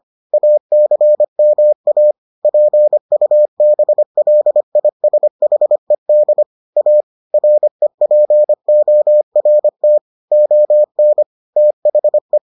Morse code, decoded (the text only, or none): ACMA PUBLISHED A REPORT ON THE